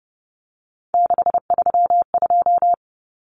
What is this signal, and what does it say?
Morse code, heard as a keyed tone, about 30 wpm, 700 Hz. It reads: 632
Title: Common Cut Numbers - 3-Digit 30wpm